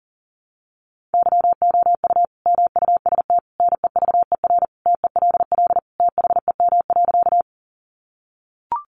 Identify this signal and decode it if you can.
Morse code, heard as a keyed tone, about 40 wpm, 700 Hz. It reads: YOU MUST NEVER TELL THEM.
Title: Sentences from Top 900 Words 40wpm